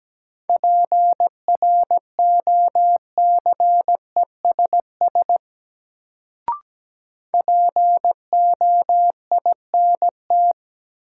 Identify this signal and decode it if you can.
Morse code, heard as a keyed tone, about 17 wpm, 700 Hz. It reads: PROCESS POINT
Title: Sets of 3 Words - New Word Drill - 17wpm